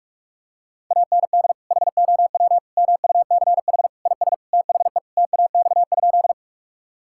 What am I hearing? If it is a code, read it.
Morse code, heard as a keyed tone, about 46 wpm, 690 Hz. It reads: AND HOW MUCH IS THE TAX?
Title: Sentences from Top 1000 Words - Review Part 5 - 45wpm